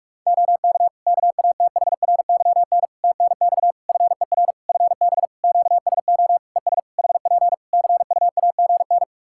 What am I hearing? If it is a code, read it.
Morse code, heard as a keyed tone, about 41 wpm, 700 Hz. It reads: OK KATHRYN TNX FER FB QSO ES HP CUAGN
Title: K1USN SST QSO Practice 45wpm